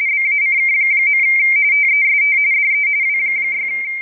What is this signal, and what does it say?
RTTY, about 45.45 baud. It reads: BACK. TEST DE W6
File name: rtty / websdr recording 14086.0kHz